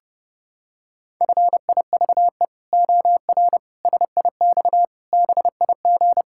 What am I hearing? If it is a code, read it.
Morse code, heard as a keyed tone, about 30 wpm, 700 Hz. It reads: FIVE OR SIX BIG
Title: Pangram Challenge v2 - 30wpm